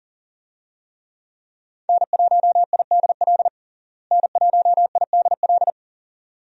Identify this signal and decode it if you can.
Morse code, heard as a keyed tone, about 40 wpm, 700 Hz. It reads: N1IDL N1IDL
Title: USA Call Signs - 5 Characters - Repeated 2x - 40wpm